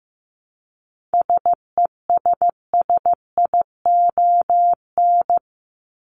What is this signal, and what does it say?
Morse code, heard as a keyed tone, about 15 wpm, 700 Hz. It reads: SESSION
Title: Top 1501-1750 Words 15wpm